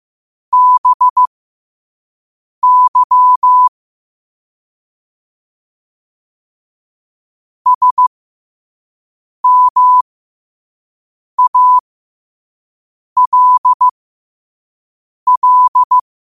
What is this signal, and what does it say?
Morse code, heard as a keyed tone, about 15 wpm, spaced as for 5 wpm, 1000 Hz. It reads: BY SMALL
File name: qotdfiles / 5wpm